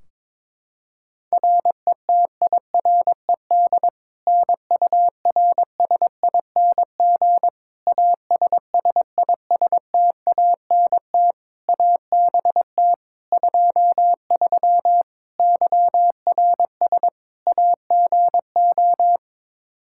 Morse code, heard as a keyed tone, about 21 wpm, 700 Hz. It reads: RETIRED NURSING ASSISTANT ABT 23 YRS AGO
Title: Practice QSO - Exchange 3 - More Info - 22wpm